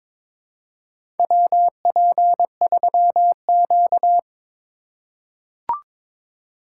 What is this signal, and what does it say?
Morse code, heard as a keyed tone, about 23 wpm, 700 Hz. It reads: WP3Q
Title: USA Call Signs - 4 Characters - 4x Character Spacing - 22wpm